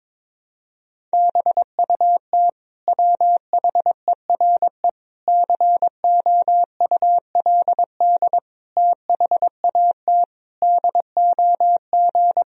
Morse code, heard as a keyed tone, about 22 wpm, 700 Hz. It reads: BUT WHERE COULD THAT DOG
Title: Sentences from Top 1000 Words - Review Part 2 - 22wpm